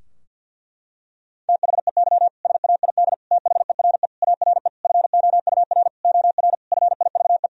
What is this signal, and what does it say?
Morse code, heard as a keyed tone, about 47 wpm, 700 Hz. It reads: THEY SAID THERE ARE FOUR OR FIVE
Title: Sentences from Top 1000 Words - Review Part 2 - 50wpm